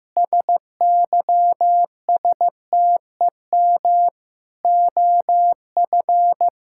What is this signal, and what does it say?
Morse code, heard as a keyed tone, about 15 wpm, 700 Hz. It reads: SYSTEM OF
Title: Sentences from Top 1000 Words - Review Part 6 - 15wpm